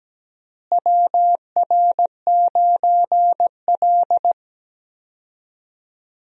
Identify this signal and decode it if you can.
Morse code, heard as a keyed tone, about 17 wpm, 700 Hz. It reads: WR9L
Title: USA Call Signs - 4 Characters - Repeated 3x - 17wpm